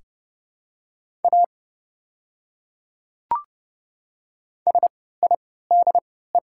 Morse code, heard as a keyed tone, about 30 wpm, spaced as for 17 wpm, 700 Hz. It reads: A SIDE
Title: 3x Character Spacing - Top 100 Words - 30wpm